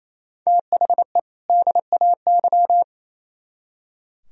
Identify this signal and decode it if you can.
Morse code, heard as a keyed tone, about 35 wpm, 700 Hz. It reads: THE DAY
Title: Speed-Racing - Sets of 2 Words 28wpm